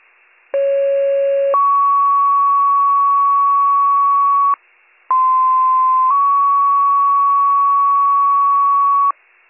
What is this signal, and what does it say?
Two-tone page, heard as a one-second tone, then a three-second tone, about 550 then 1120 Hz, about 1040 then 1120 Hz. It is heard from 0.5 to 9.1 s.